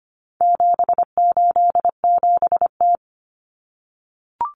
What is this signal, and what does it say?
Morse code, heard as a keyed tone, about 25 wpm, 700 Hz. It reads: 787T
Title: Common Cut Numbers - 4-Digit 25wpm